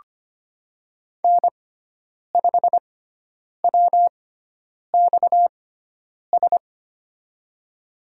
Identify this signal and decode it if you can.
Morse code, heard as a keyed tone, about 25 wpm, spaced as for 9 wpm, 700 Hz. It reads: N5WXS
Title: USA Call Signs - 6x Character Spacing - 25wpm